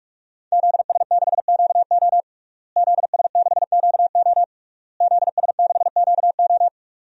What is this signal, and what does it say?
Morse code, heard as a keyed tone, about 47 wpm, 700 Hz. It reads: ZS6QO ZS6QO ZS6QO
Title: DXCC-100 Call Signs - 5 Characters - Repeated 3x - 45wpm